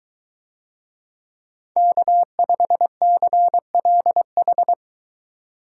Morse code, heard as a keyed tone, about 21 wpm, 700 Hz. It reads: K5CLH